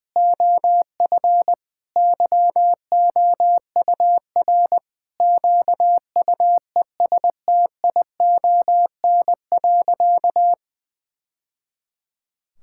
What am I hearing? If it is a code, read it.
Morse code, heard as a keyed tone, about 21 wpm, 700 Hz. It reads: OF YOUR QUESTION.
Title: Sentences from Top 1000 Words - Review Part 6 - 20wpm